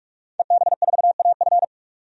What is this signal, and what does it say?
Morse code, heard as a keyed tone, about 46 wpm, 700 Hz. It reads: EB4AF
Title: DXCC-100 Call Signs - 5 Characters - 45wpm